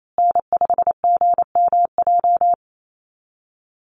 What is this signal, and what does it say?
Morse code, heard as a keyed tone, about 28 wpm, 700 Hz. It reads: N5GMJ
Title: USA Call Signs Repeated 2x 28wpm